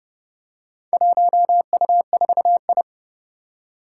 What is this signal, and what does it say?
Morse code, heard as a keyed tone, about 30 wpm, 700 Hz. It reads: 1U4I
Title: Four Challenging Letter-Number Combinations 30wpm